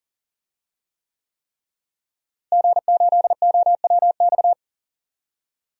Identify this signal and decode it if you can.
Morse code, heard as a keyed tone, about 40 wpm, 700 Hz. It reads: G8OWX
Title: DXCC-100 Call Signs - 5 Characters - Repeated 3x - 40wpm